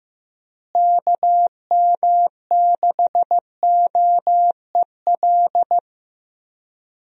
Morse code, heard as a keyed tone, about 15 wpm, 700 Hz. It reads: KM6OEL
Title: USA Call Signs - 6 Characters - Repeated 3x - 15wpm